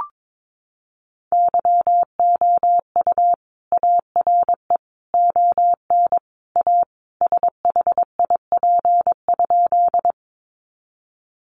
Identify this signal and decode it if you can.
Morse code, heard as a keyed tone, about 22 wpm, 700 Hz. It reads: YOU ARE ON A SHIP?
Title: Sentences from Top 400 Words 22wpm